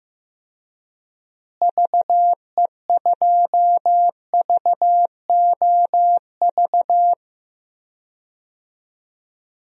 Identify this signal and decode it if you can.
Morse code, heard as a keyed tone, about 15 wpm, 700 Hz. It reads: VE2VOV
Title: DXCC-100 Call Signs - 6 Characters - Repeated 3x - 15wpm